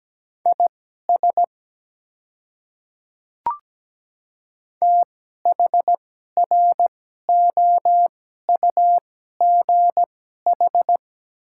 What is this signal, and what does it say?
Morse code, heard as a keyed tone, about 17 wpm, spaced as for 12 wpm, 700 Hz. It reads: IS THROUGH
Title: 2x Character Spacing - Top 200 Words - 17wpm